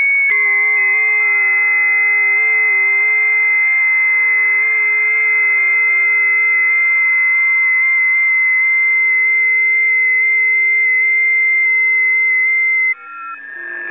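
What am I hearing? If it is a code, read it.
FT8.
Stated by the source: FT8 signals recorded on a busy 20m band.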